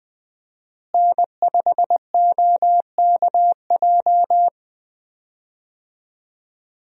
Morse code, heard as a keyed tone, about 20 wpm, 700 Hz. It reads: N5OKJ
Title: USA Call Signs - 5 Characters - Repeated 2x - 20wpm